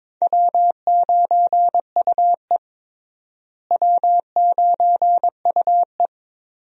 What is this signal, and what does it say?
Morse code, heard as a keyed tone, about 21 wpm, 700 Hz. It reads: W9UE W9UE
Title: USA Call Signs - 4 Characters - Repeated 3x - 22wpm